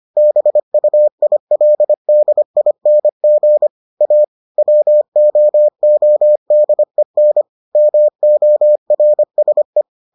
Morse code, heard as a keyed tone, about 25 wpm, 600 Hz. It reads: BUILDING A WOODEN MORSE